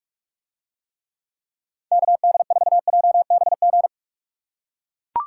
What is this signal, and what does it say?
Morse code, heard as a keyed tone, about 41 wpm, 700 Hz. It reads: KD4JBG